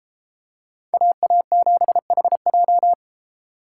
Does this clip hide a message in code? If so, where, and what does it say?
Yes: Morse code from 0.9 to 2.9 s, heard as a keyed tone, about 33 wpm, 700 Hz. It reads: AA7HJ